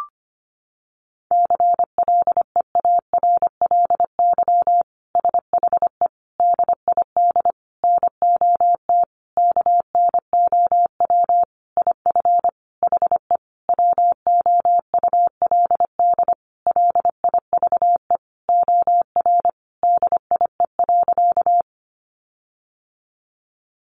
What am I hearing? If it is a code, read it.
Morse code, heard as a keyed tone, about 25 wpm, 700 Hz. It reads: CLEARLY SHE DID NOT KNOW IF HE WOULD LIVE OR DIE.
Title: Sentences from Top 1300 Words 25wpm